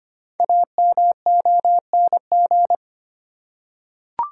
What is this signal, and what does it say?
Morse code, heard as a keyed tone, about 25 wpm, 700 Hz. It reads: AMONG
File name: 5 Letter Words - First 3 Letters Given - 25wpm